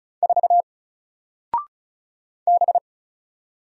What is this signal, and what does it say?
Morse code, heard as a keyed tone, about 36 wpm, 700 Hz. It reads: 4 B
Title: Single Letter-Number Alt 35wpm